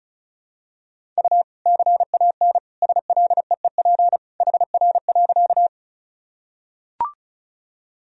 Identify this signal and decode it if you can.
Morse code, heard as a keyed tone, about 36 wpm, 700 Hz. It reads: U CAN SLEEP HR.
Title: Sentences from Top 500 Words and Abbreviations 35wpm